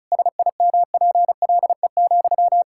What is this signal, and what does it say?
Morse code, heard as a keyed tone, about 36 wpm, 700 Hz. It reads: SIMPLE,
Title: Sentences from Top 1000 Words - Review Part 6 - 35wpm